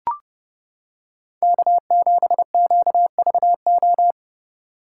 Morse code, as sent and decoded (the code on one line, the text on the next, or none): -.- --... --.- ...- ---
K7QVO